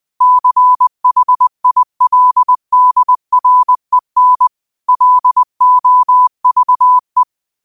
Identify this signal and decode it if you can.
Morse code, heard as a keyed tone, about 20 wpm, 1000 Hz. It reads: CHILDREN LOVE
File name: qotdfiles / 20wpm